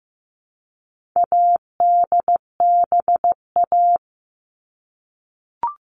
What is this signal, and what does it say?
Morse code, heard as a keyed tone, about 15 wpm, 700 Hz. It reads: ADBA
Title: All Cut Numbers - 4-Digit 15wpm